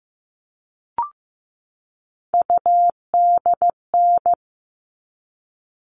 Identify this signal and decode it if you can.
Morse code, heard as a keyed tone, about 15 wpm, 700 Hz. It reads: UDN
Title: All Cut Numbers - 3-Digit 15wpm